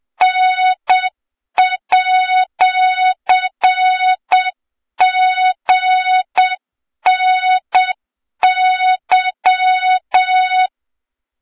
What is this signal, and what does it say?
Morse code, heard as a keyed tone, about 7 wpm, 3700 Hz, with noise. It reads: N@GNY
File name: Morze-HA2MN / hang / 23 GYAKORLAT